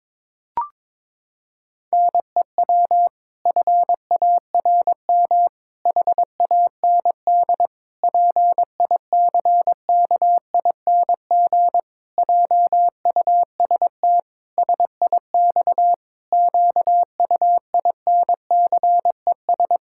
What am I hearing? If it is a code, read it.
Morse code, heard as a keyed tone, about 22 wpm, 700 Hz. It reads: NEW FARM HAND PICKING JUST SIX QUINCES
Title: Pangram Challenge v2 - 22wpm